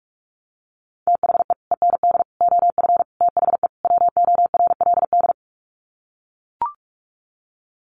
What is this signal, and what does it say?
Morse code, heard as a keyed tone, about 44 wpm, 700 Hz. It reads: THE END OF THE WORLD
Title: Sets of 5 Words 45wpm